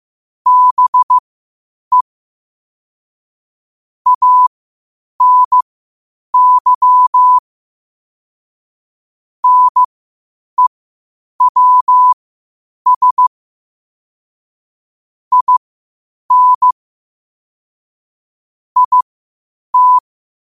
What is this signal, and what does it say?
Morse code, heard as a keyed tone, about 15 wpm, spaced as for 8 wpm, 1000 Hz. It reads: BE ANY NEWS IN IT
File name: qotdfiles / 8wpm